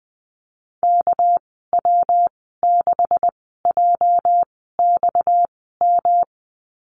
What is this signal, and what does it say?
Morse code, heard as a keyed tone, about 20 wpm, spaced as for 15 wpm, 700 Hz. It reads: KW6JXM
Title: 245 Lesson 39g - Letter z - Call Signs 20wpm